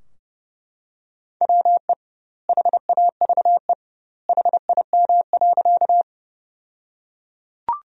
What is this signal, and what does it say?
Morse code, heard as a keyed tone, about 32 wpm, 700 Hz. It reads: WE HAVE HIM.
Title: 2x Word-Spacing - Sentences from Top 100 Words - 30wpm